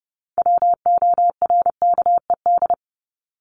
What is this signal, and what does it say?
Morse code, heard as a keyed tone, about 30 wpm, 700 Hz. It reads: WORKED
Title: ED Suffix - Top 100 Words with Base Words 30wpm